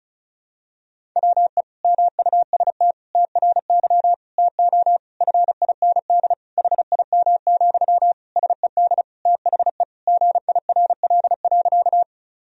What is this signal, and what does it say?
Morse code, heard as a keyed tone, about 34 wpm, 700 Hz. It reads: WE MUST TRY TO FIND HIM, SED THE GIRL.
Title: Sentences from Top 400 Words and Abbreviations 35wpm